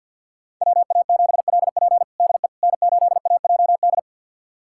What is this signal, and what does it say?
Morse code, heard as a keyed tone, about 46 wpm, 700 Hz. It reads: WA7LP DE N8AJD
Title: USA Callsign DE Callsign - 5 Characters - 50wpm